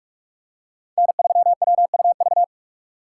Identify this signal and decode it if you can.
Morse code, heard as a keyed tone, about 46 wpm, 700 Hz. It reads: N3WUV
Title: USA Call Signs - 5 Characters - 4x Character Spacing - 45wpm